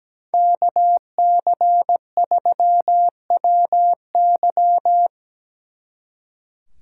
Morse code, heard as a keyed tone, about 20 wpm, 700 Hz. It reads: KC3WY
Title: Speed Racing - USA 2x2 Call Signs 17wpm